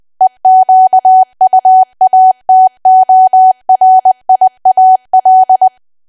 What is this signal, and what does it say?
Morse code, heard as a keyed tone, about 20 wpm, 750 Hz. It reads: EQUATORIAL
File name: cwa / W1AW-Level3 / 20WPM